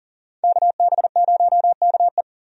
Morse code, heard as a keyed tone, about 40 wpm, 700 Hz. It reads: KB0KE
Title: USA Call Signs - 5 Characters - 40wpm